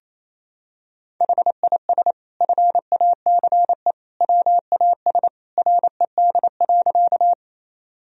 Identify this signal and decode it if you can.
Morse code, heard as a keyed tone, about 28 wpm, 700 Hz. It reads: HIS FACE WAS RED.
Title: Sentences from Top 1000 Words - Review Part 2 - 28wpm